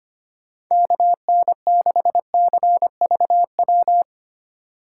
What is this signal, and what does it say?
Morse code, heard as a keyed tone, about 25 wpm, 700 Hz. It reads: KN6CVW
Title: USA Call Signs - 6 Characters - 6x Character Spacing - 25wpm